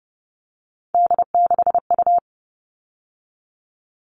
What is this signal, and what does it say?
Morse code, heard as a keyed tone, about 30 wpm, 700 Hz. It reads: D6U
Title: All Cut Numbers - 3-Digit 30wpm